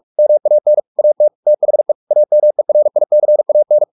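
Morse code, heard as a keyed tone, about 44 wpm, 600 Hz. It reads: MAN AT THE AMERICAN
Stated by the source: Morse practice at 45 wpm, the novel Never by Ken Follett in English, from CW med Gnister.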